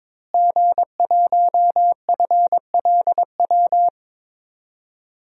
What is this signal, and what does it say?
Morse code, heard as a keyed tone, about 22 wpm, 700 Hz. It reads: G1FLW
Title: DXCC-100 Call Signs - 5 Characters - Repeated 3x - 22wpm